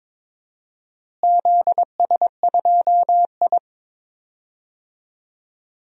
Morse code, heard as a keyed tone, about 22 wpm, 700 Hz. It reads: ZS2I
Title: DXCC-100 Call Signs - 4 Characters - Repeated 3x - 22wpm